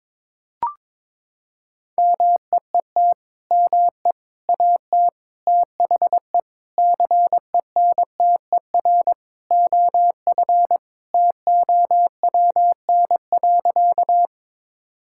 Morse code, heard as a keyed tone, about 22 wpm, 700 Hz. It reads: MEET ME AT THE CENTER OF TOWN.
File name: practice / 22wpm / Sentences with 3-Word Phrases - 22wpm